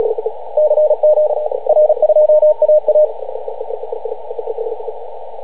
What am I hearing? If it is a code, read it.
Morse code, heard as a keyed tone, about 37 wpm, 610 Hz, with radio noise. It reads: CM R2AA